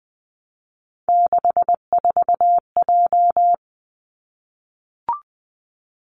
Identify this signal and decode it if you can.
Morse code, heard as a keyed tone, about 20 wpm, 700 Hz. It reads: 64J
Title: Three Challenging Letter-Number Combinations 20wpm